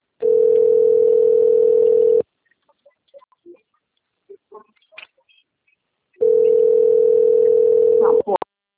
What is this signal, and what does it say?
Ringback tone (North America), 440 + 480 Hz, with speech.
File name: rg-1600-unknown- - - .19776